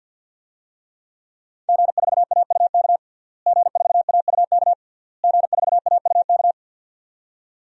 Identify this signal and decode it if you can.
Morse code, heard as a keyed tone, about 46 wpm, 700 Hz. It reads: G4AUK G4AUK G4AUK
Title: DXCC-100 Call Signs - 5 Characters - Repeated 3x - 50wpm